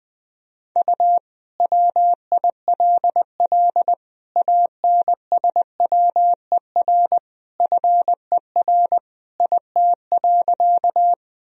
Morse code, heard as a keyed tone, about 20 wpm, 700 Hz. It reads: U WILL ANSWER FER IT.
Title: Sentences from Top 300 Words and Abbreviations 20wpm